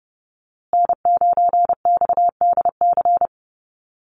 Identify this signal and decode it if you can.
Morse code, heard as a keyed tone, about 30 wpm, 700 Hz. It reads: N9XDC
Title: USA Call Signs Repeated 2x 30wpm